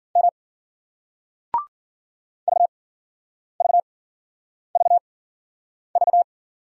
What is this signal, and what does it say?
Morse code, heard as a keyed tone, about 55 wpm, 700 Hz. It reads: M V V V V